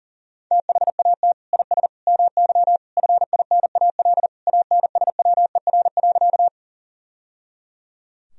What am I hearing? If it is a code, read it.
Morse code, heard as a keyed tone, about 37 wpm, 700 Hz. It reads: THAT IS MY FINAL ANSWER.
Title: Sentences from Top 1000 Words - Review Part 2 - 40wpm